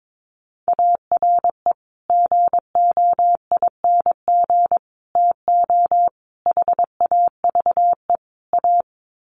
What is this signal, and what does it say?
Morse code, heard as a keyed tone, about 22 wpm, 700 Hz. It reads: ARE GOING TO HAVE A
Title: Sets of 5 Words Encore2 22wpm